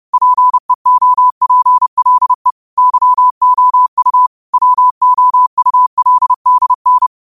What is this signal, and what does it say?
Morse code, heard as a keyed tone, about 30 wpm, 1000 Hz. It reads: PEOPLE YOU WOULDN
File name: qotdfiles / 30wpm